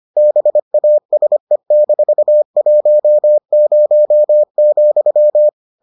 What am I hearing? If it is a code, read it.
Morse code, heard as a keyed tone, about 25 wpm, 600 Hz. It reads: BASE-10,